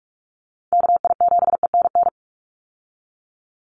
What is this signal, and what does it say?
Morse code, heard as a keyed tone, about 44 wpm, 700 Hz. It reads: KI7ENN
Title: USA Call Signs 45wpm